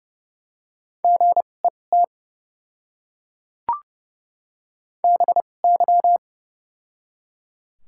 Morse code, heard as a keyed tone, about 30 wpm, spaced as for 22 wpm, 700 Hz. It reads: GET BY
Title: 2x Character Spacing - Top 500 Words - 30wpm